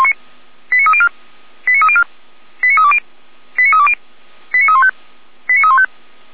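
Five-tone selective call (ZVEI), heard as quick runs of five tones, tones of about 70 ms. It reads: 89304 89304 89320 89320 89315 89315
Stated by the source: ZVEI five-tone selective call received with an SDR, from the Feuerwehr Orlamünde audio server.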